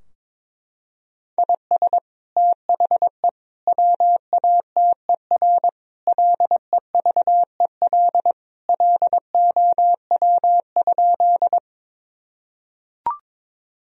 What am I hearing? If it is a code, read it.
Morse code, heard as a keyed tone, about 21 wpm, 700 Hz. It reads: IS THE WATER LEVEL LOW?
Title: Sentences from Top 1000 Words - Review Part 4 - 22wpm